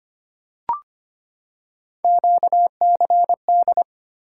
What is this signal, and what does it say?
Morse code, heard as a keyed tone, about 25 wpm, 700 Hz. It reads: QCD
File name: Three Letter-Number Combinations - 25wpm